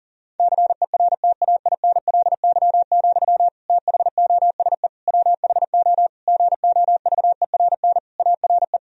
Morse code, heard as a keyed tone, about 40 wpm, 700 Hz. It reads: CERTAINLY, THOSE WHO GOVERN ARE
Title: Sentences from Top 1000 Words - Review Part 2 - 40wpm